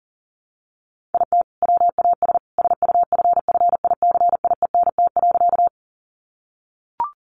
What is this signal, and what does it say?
Morse code, heard as a keyed tone, about 40 wpm, 700 Hz. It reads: IT WAS SUFFICIENT.
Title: Sentences from Top 1400 Words 40wpm